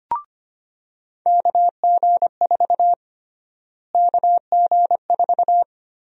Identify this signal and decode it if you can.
Morse code, heard as a keyed tone, about 25 wpm, 700 Hz. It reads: KG4 KG4
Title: USA Call Sign Prefixes - Repeated 2x - 25wpm